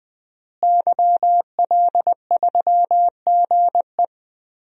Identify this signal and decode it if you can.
Morse code, heard as a keyed tone, about 20 wpm, 700 Hz. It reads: YL3GE
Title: DXCC-100 Call Signs - 5 Characters - Repeated 3x - 20wpm